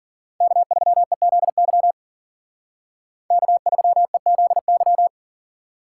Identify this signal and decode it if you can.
Morse code, heard as a keyed tone, about 48 wpm, 700 Hz. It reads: K3EZY K3EZY
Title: Speed Racing - USA 1x3 Call Signs 40wpm